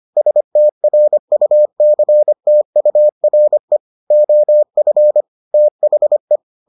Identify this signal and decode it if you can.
Morse code, heard as a keyed tone, about 25 wpm, 600 Hz. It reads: STRUCTURE OF THE